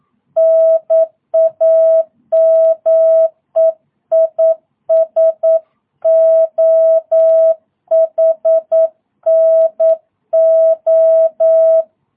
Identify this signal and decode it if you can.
Morse code, heard as a keyed tone, about 9 wpm, 660 Hz. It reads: NAME IS OHNO
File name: jh8cbh / 2022cw